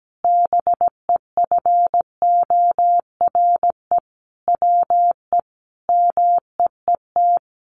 Morse code, heard as a keyed tone, about 17 wpm, 700 Hz. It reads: BEFORE WE MEET
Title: Sentences from Top 1100 Words 17wpm